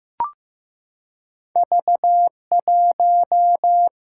Morse code, heard as a keyed tone, about 15 wpm, 700 Hz. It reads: V1
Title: Single-Letter Single-Number 15wpm